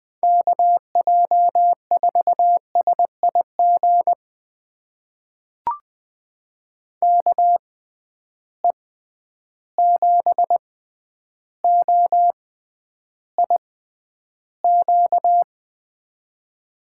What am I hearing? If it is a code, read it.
Morse code, heard as a keyed tone, about 20 wpm, spaced as for 10 wpm, 700 Hz. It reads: KJ4SIG KE7OIQ
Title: USA Call Signs - 6 Characters - 6x Character Spacing - 20wpm